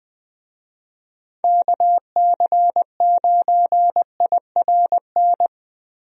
Morse code, heard as a keyed tone, about 20 wpm, 700 Hz. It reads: KC9IRN
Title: USA Call Signs - 6 Characters - Repeated 3x - 20wpm